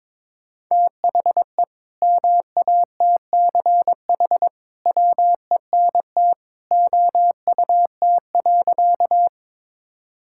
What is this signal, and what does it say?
Morse code, heard as a keyed tone, about 21 wpm, 700 Hz. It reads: THE MATCH WENT OUT.
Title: Sentences from Top 1600 Words 22wpm